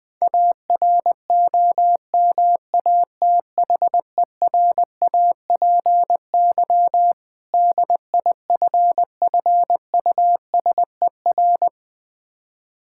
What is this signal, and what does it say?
Morse code, heard as a keyed tone, about 20 wpm, 690 Hz. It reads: AROMATHERAPY DIFFUSER
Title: Difficult Words with Hints - 20wpm